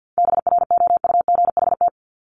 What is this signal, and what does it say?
Morse code, heard as a keyed tone, about 50 wpm, 700 Hz. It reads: BROUGHT
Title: Top 500 Words 50wpm